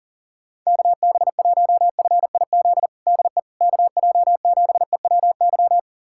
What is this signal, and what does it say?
Morse code, heard as a keyed tone, about 40 wpm, 700 Hz. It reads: KB1FIZ DE KJ7EWY
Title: USA Callsign DE Callsign 40wpm